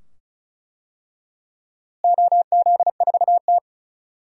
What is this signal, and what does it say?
Morse code, heard as a keyed tone, about 34 wpm, 700 Hz. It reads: OZ4T
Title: DXCC-100 Call Signs - 4 Characters - Repeated 2x - 35wpm